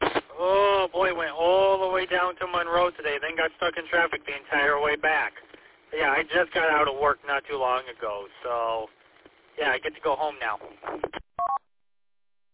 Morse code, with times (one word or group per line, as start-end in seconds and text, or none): none